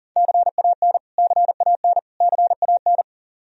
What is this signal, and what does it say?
Morse code, heard as a keyed tone, about 40 wpm, 700 Hz. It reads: CAN CAN CAN
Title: Top 100 Words Repeated 40wpm